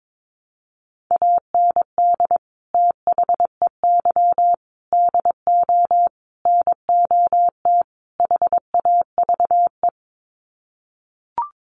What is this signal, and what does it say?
Morse code, heard as a keyed tone, about 22 wpm, 700 Hz. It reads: AND THEY DO NOT HAVE
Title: Sets of 5 Words Encore2 22wpm